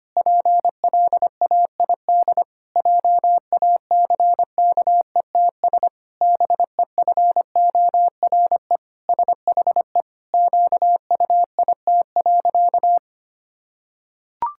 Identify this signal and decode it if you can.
Morse code, heard as a keyed tone, about 25 wpm, 700 Hz. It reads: PLAID JACKETS BEFORE SHE QUIT.
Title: Pangram Challenge v2 - 25wpm